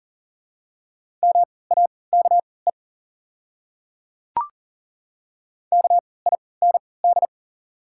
Morse code, heard as a keyed tone, about 37 wpm, spaced as for 22 wpm, 700 Hz. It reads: MAKE KIND
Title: 3x Character Spacing - Top 300 Words - 40wpm